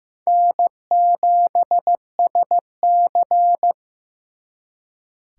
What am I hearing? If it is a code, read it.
Morse code, heard as a keyed tone, about 18 wpm, 700 Hz. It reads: N7SC
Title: Speed Racing - USA 1x2 Call Signs 15wpm